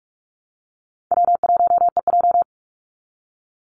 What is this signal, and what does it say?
Morse code, heard as a keyed tone, about 44 wpm, 700 Hz. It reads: W1EJ
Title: NA SOTA Activator Callsigns 45wpm